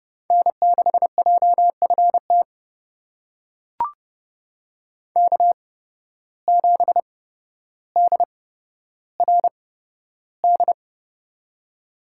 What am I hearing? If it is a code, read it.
Morse code, heard as a keyed tone, about 30 wpm, spaced as for 13 wpm, 700 Hz. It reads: N6JFT K7DRD
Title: USA Call Signs - 5 Characters - 8x Character Spacing - 30wpm